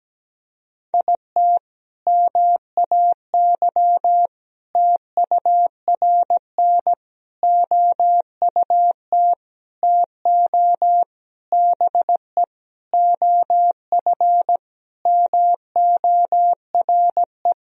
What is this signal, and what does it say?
Morse code, heard as a keyed tone, about 17 wpm, 700 Hz. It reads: IT MAY TURN OUT TO BE OF MORE
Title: Sentences from Top 400 Words and Abbreviations 17wpm